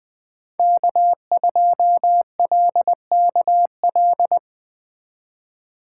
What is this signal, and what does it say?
Morse code, heard as a keyed tone, about 21 wpm, 700 Hz. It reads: K2LKL